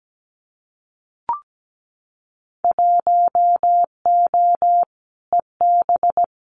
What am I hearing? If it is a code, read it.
Morse code, heard as a keyed tone, about 17 wpm, 700 Hz. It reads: 1O EB
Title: Winter Field Day Exchanges 17wpm